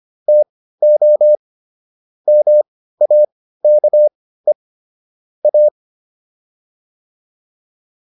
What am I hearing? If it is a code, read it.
Morse code, heard as a keyed tone, about 25 wpm, spaced as for 15 wpm, 600 Hz. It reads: TO MAKE A